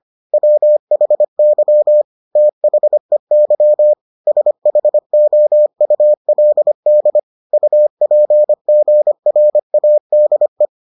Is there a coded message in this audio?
Yes: Morse code from 0.3 to 10.7 s, heard as a keyed tone, about 25 wpm, 600 Hz. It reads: WHY THEY SHOULD UPGRADE